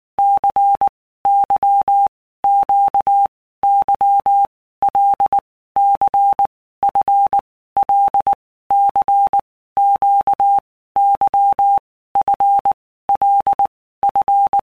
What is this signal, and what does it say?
Morse code, heard as a keyed tone, about 19 wpm, 800 Hz. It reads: C Y Q Y L C F L C Q Y F L F
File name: Basic Letter Groups By Blocks / Groups@18 / LFCYQ@18